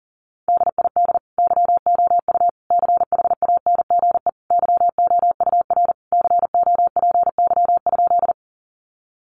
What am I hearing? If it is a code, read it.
Morse code, heard as a keyed tone, about 40 wpm, 700 Hz. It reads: DID YOU CHANGE YOUR COPY?
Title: Sentences from Top 1400 Words 40wpm